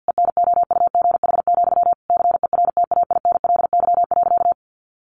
Morse code, heard as a keyed tone, about 50 wpm, 700 Hz. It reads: ENOUGH, CERTAINLY.
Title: Sentences from Top 900 Words 50wpm